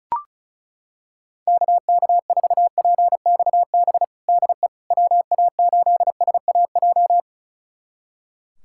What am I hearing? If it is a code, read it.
Morse code, heard as a keyed tone, about 34 wpm, 700 Hz. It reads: KK4PXB DE WA8SAJ
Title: USA Callsign DE Callsign - 6 Characters - 35wpm